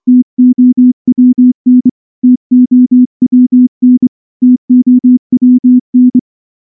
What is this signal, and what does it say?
Morse code, heard as a keyed tone, about 24 wpm, 250 Hz. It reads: TOWN TOWN TOWN
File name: morse / morse-training-25WPM